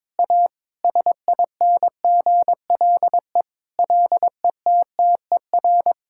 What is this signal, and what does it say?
Morse code, heard as a keyed tone, about 22 wpm, 700 Hz. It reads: A SINGLE LETTER
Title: Sentences from Top 1600 Words 22wpm